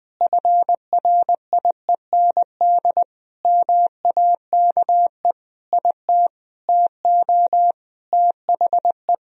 Morse code, heard as a keyed tone, about 19 wpm, 700 Hz. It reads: FRIEND MAKE IT TO THE
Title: Sentences from Top 1000 Words - Review Part 3 - 20wpm